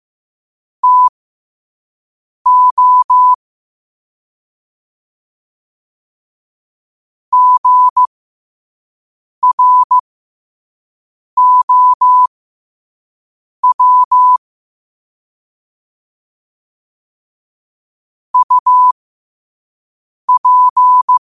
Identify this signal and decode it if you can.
Morse code, heard as a keyed tone, about 15 wpm, spaced as for 5 wpm, 1000 Hz. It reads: TO GROW UP
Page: 5 WPM morse code quotes for Wed, 30 Jul 2025.